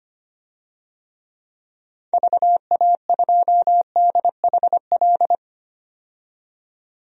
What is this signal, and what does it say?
Morse code, heard as a keyed tone, about 25 wpm, 700 Hz. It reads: VA2DHL
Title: DXCC-100 Call Signs - 6 Characters - 25wpm